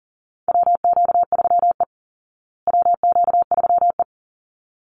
Morse code, heard as a keyed tone, about 40 wpm, 700 Hz. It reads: WQ3E WQ3E
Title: Call Signs Repeated 3x - 40wpm